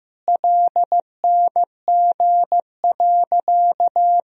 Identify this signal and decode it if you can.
Morse code, heard as a keyed tone, about 15 wpm, 700 Hz. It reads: LNG.
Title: Sentences from Top 100 Words and Abbreviations 15wpm